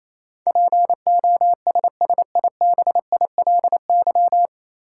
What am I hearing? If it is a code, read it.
Morse code, heard as a keyed tone, about 28 wpm, 700 Hz. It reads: POSSIBILY
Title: Sentences from Top 1300 Words - New Word Drill Alt - 28wpm